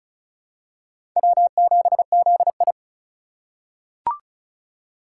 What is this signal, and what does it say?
Morse code, heard as a keyed tone, about 36 wpm, 700 Hz. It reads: W7ZI
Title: USA Call Signs - 4 Characters - 3x Character Spacing - 35wpm